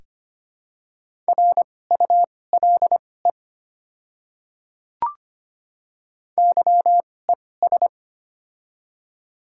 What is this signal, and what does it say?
Morse code, heard as a keyed tone, about 25 wpm, spaced as for 18 wpm, 700 Hz. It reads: RULE YES
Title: 2x Character Spacing - Top 500 Words - 25wpm